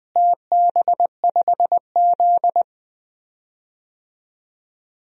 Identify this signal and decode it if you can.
Morse code, heard as a keyed tone, about 20 wpm, 700 Hz. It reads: TB5Z
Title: DXCC-100 Call Signs - 4 Characters - Repeated 2x - 20wpm